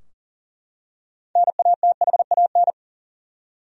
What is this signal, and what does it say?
Morse code, heard as a keyed tone, about 40 wpm, 700 Hz. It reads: NATHAN
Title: USA Common Names - 40wpm